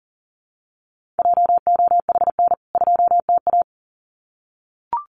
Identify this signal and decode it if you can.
Morse code, heard as a keyed tone, about 40 wpm, 700 Hz. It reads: JOHN 2TA
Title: CWT Exchange 40wpm